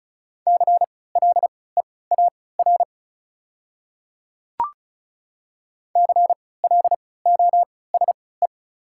Morse code, heard as a keyed tone, about 36 wpm, spaced as for 20 wpm, 690 Hz. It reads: CLEAR CLOSE
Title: 3x Character Spacing - Top 500 Words - 35wpm